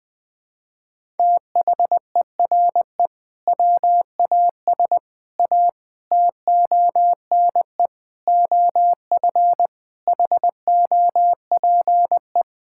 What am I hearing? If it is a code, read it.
Morse code, heard as a keyed tone, about 20 wpm, 700 Hz. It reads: THERE WAS A TONE OF HOPE
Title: Sentences from Top 1600 Words 20wpm